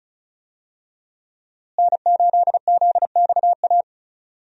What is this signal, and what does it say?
Morse code, heard as a keyed tone, about 36 wpm, 700 Hz. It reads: N8ZXA
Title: USA Call Signs - 5 Characters - Repeated 3x - 35wpm